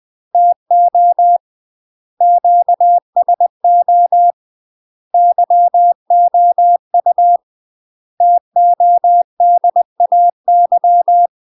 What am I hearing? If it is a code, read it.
Morse code, heard as a keyed tone, about 20 wpm, 700 Hz. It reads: TO QSO YOU TODAY
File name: UK MASTER QSOs TRAINING / 20WPM / UK MASTER QSOs 0052